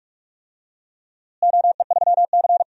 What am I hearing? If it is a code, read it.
Morse code, heard as a keyed tone, about 46 wpm, 700 Hz. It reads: OE3C
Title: DXCC-100 Call Signs - 4 Characters - Repeated 3x - 45wpm